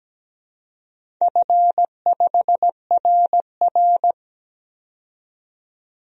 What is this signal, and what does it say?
Morse code, heard as a keyed tone, about 17 wpm, 700 Hz. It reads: F5RR
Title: DXCC-100 Call Signs - 4 Characters - 17wpm